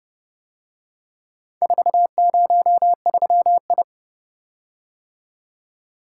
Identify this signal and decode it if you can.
Morse code, heard as a keyed tone, about 30 wpm, 700 Hz. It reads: 403I